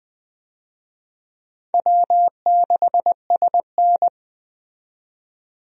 Morse code, heard as a keyed tone, about 20 wpm, 700 Hz. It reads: W6SN
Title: USA Call Signs - 4 Characters - 20wpm